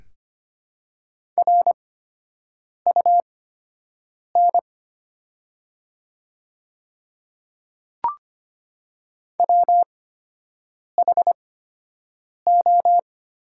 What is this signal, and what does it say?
Morse code, heard as a keyed tone, about 25 wpm, spaced as for 7 wpm, 700 Hz. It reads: RUN WHO
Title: 8x Character Spacing - Top 500 Words - 25wpm